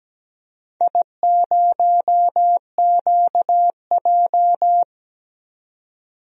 Morse code, heard as a keyed tone, about 17 wpm, 700 Hz. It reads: I0QJ
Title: DXCC-100 Call Signs - 4 Characters - 17wpm